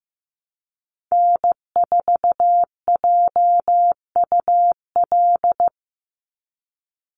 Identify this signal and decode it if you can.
Morse code, heard as a keyed tone, about 15 wpm, 700 Hz. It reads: N4JUL
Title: USA Call Signs 15wpm